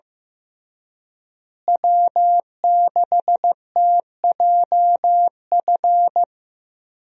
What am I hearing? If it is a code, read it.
Morse code, heard as a keyed tone, about 16 wpm, 700 Hz. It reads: W6TJF
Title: USA Call Signs - 5 Characters - Repeated 3x - 15wpm